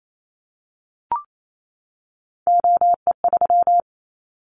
Morse code, heard as a keyed tone, about 28 wpm, 700 Hz. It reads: OE3
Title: Callsign Prefixes for Top 25 DXCC - 28wpm